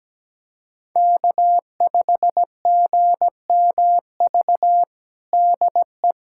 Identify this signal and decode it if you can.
Morse code, heard as a keyed tone, about 17 wpm, 700 Hz. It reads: K5GMV DE
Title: USA Callsign DE Callsign - 5 Characters - 17wpm